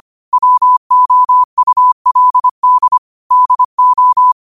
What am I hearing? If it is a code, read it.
Morse code, heard as a keyed tone, about 25 wpm, 1000 Hz. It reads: WOULD DO
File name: qotdfiles / 25wpm